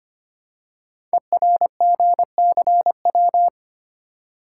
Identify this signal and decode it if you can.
Morse code, heard as a keyed tone, about 25 wpm, 700 Hz. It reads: ERGCW
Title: Code Groups - 5 Letter:Numbers - 25wpm